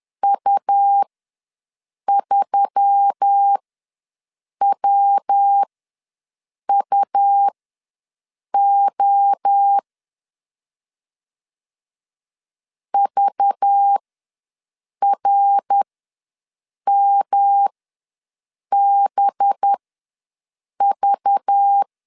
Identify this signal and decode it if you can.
Morse code, heard as a keyed tone, about 10 wpm, spaced as for 6 wpm, 800 Hz. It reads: U3WUO VRMBV
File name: LEZIONI / 5 LBVC / 5 LBV ws 10wpm 3645